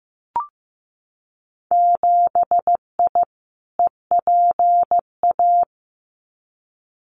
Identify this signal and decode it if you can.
Morse code, heard as a keyed tone, about 15 wpm, 700 Hz. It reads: 7I EPA
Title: Winter Field Day Exchanges Encore 15wpm